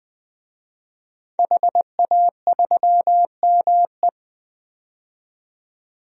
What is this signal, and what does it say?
Morse code, heard as a keyed tone, about 20 wpm, 700 Hz. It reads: HA3ME
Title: DXCC-100 Call Signs - 5 Characters - 20wpm